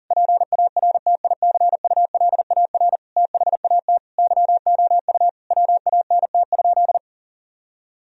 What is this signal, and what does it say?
Morse code, heard as a keyed tone, about 40 wpm, 700 Hz. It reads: PARTICULAR THAT YOU WANT?
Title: Sentences from Top 1000 Words - Review Part 4 - 40wpm